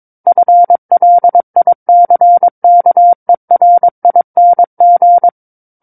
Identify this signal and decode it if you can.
Morse code, heard as a keyed tone, about 22 wpm, 700 Hz. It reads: FLICKERING
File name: Book One / 13-Chapter 13 - 22 WPM